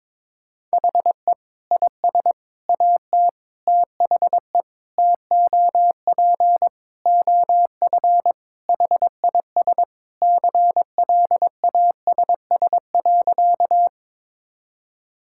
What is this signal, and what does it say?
Morse code, heard as a keyed tone, about 22 wpm, 700 Hz. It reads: HE IS AT THE TOP OF HIS CLASS.
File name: practice / 22wpm / Sentences with 4-Word Phrases - 22wpm